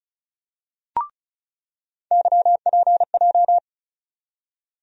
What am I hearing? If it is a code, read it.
Morse code, heard as a keyed tone, about 36 wpm, 700 Hz. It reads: YPJ
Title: USA Call Sign Postfixes - 3 Letters 35wpm